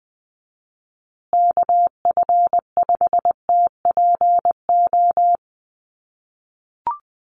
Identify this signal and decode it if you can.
Morse code, heard as a keyed tone, about 20 wpm, 700 Hz. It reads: KF5TPO
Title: USA Call Signs Repeated 2x 20wpm